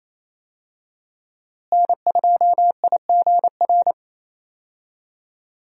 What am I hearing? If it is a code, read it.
Morse code, heard as a keyed tone, about 28 wpm, 700 Hz. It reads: N2IGR
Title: USA Call Signs - 5 Characters - Repeated 2x - 28wpm